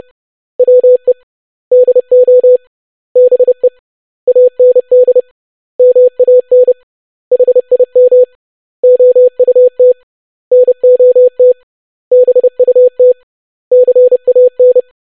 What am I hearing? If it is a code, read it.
Morse code, heard as a keyed tone, about 30 wpm, 500 Hz. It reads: WE DO BE AND MAN HIM OUT NOT BUT CAN
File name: basic100-25wpm